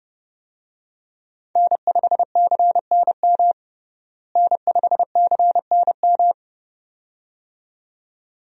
Morse code, heard as a keyed tone, about 32 wpm, 700 Hz. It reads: N5CNM N5CNM
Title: USA Call Signs - 5 Characters - Repeated 2x - 30wpm